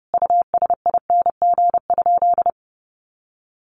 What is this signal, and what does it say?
Morse code, heard as a keyed tone, about 30 wpm, 700 Hz. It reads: USING?
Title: Common QSO Phrases 30wpm